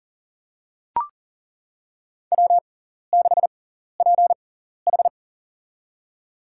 Morse code, heard as a keyed tone, about 40 wpm, spaced as for 14 wpm, 700 Hz. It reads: W6PH
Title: USA Call Signs - 4 Characters - 6x Character Spacing - 40wpm